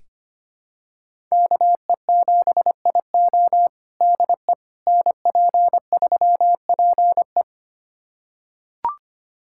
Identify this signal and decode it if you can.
Morse code, heard as a keyed tone, about 25 wpm, 700 Hz. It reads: KE7IO DE NP3PE
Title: USA Callsign DE Callsign - 5 Characters - 25wpm